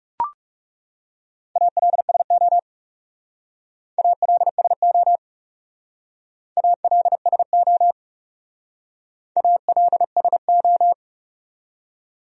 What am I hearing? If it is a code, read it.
Morse code, heard as a keyed tone, about 38 wpm, 700 Hz. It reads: ALSO ALSO ALSO ALSO